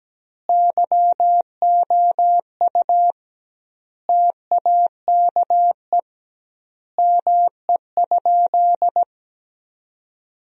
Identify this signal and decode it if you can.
Morse code, heard as a keyed tone, about 17 wpm, 700 Hz. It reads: YOU TAKE ME?
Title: 2x Word-Spacing - Sentences from Top 200 Words - 17wpm